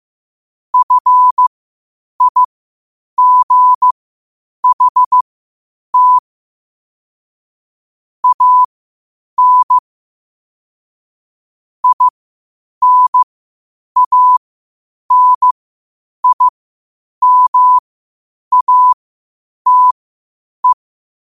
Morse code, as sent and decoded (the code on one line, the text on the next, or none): ..-. .. --. .... - / .- -. / .. -. .- -. .. -- .- - .
FIGHT AN INANIMATE